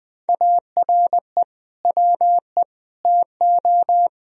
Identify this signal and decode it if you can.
Morse code, heard as a keyed tone, about 20 wpm, 700 Hz. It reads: ARE WE TO
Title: Sentences from Top 1000 Words - Review Part 1 - 20wpm